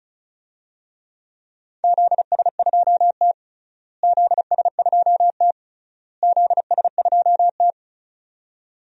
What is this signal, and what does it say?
Morse code, heard as a keyed tone, about 36 wpm, 700 Hz. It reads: ZS2T ZS2T ZS2T
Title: DXCC-100 Call Signs - 4 Characters - Repeated 3x - 35wpm